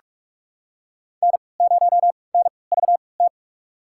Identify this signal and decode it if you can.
Morse code, heard as a keyed tone, about 43 wpm, spaced as for 25 wpm, 700 Hz. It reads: N0NVT
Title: USA Call Signs - 5 Characters - 3x Character Spacing - 45wpm